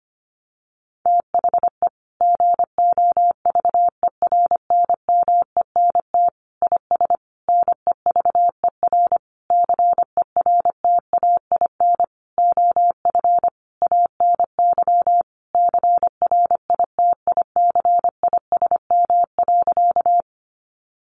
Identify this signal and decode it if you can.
Morse code, heard as a keyed tone, about 25 wpm, 700 Hz. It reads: THE GOVERNMENT IS NEVER CERTAIN OF ANY CRITICISM.
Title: Sentences from Top 1300 Words 25wpm